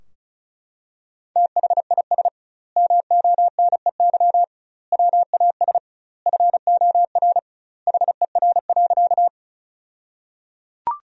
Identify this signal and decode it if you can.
Morse code, heard as a keyed tone, about 36 wpm, 700 Hz. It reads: THIS MONEY WAS FOR HER.
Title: 2x Word-Spacing - Sentences from Top 400 Words - 35wpm